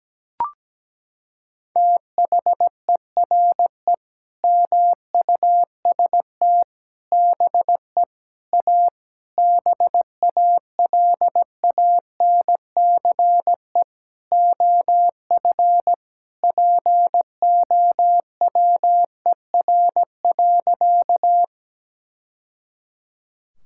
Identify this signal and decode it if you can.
Morse code, heard as a keyed tone, about 17 wpm, 700 Hz. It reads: THERE MUST BE A BALANCE OF POWER.
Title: Sentences from Top 1000 Words - Review Part 6 - 17wpm